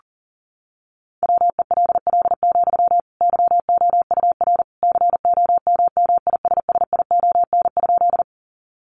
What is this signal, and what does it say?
Morse code, heard as a keyed tone, about 40 wpm, 700 Hz. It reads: WELL, YOUR COMMISSION?
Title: Sentences from Top 1400 Words 40wpm